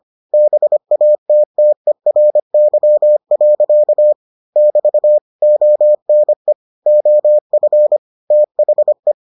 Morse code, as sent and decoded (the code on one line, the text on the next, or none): -... .- - - . .-. -.-- .-.-.- / -...- / --- -. / --- ..-. / - .... .
BATTERY. <BT> ONE OF THE